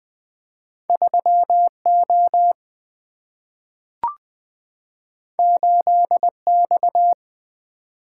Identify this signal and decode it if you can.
Morse code, heard as a keyed tone, about 20 wpm, 700 Hz. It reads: 3 O 8 X
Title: Single-Letter Single-Number 20wpm